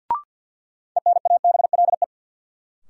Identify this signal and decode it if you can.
Morse code, heard as a keyed tone, about 49 wpm, 700 Hz. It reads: ENABLE
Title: Sentences from Top 1500 Words - New Word Drill Alt - 50wpm